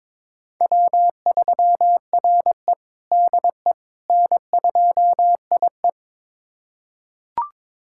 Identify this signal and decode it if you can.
Morse code, heard as a keyed tone, about 22 wpm, 700 Hz. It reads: W3RE DE N2IE
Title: USA Callsign DE Callsign - 4 Characters - 22wpm